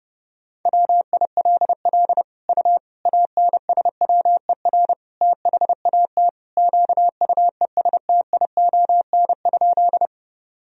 Morse code, heard as a keyed tone, about 30 wpm, 700 Hz. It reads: WILL U ANSWER THAT QUESTION?
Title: Sentences from Top 400 Words and Abbreviations 30wpm